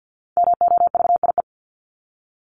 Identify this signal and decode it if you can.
Morse code, heard as a keyed tone, about 50 wpm, 700 Hz. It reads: MOVIE
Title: Top 501-750 Words 50wpm